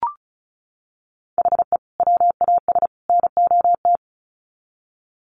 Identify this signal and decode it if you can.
Morse code, heard as a keyed tone, about 34 wpm, 700 Hz. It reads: HE WAS NOT
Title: Sets of 3 Words 35wpm